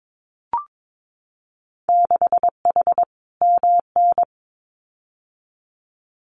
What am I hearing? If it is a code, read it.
Morse code, heard as a keyed tone, about 22 wpm, 700 Hz. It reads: 6H MN
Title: Winter Field Day Exchanges Encore 22wpm